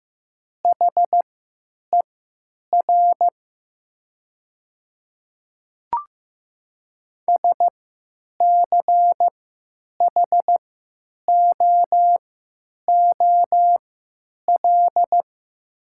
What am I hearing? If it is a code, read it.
Morse code, heard as a keyed tone, about 15 wpm, spaced as for 9 wpm, 700 Hz. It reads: HER SCHOOL
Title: 3x Character Spacing - Top 300 Words - 15wpm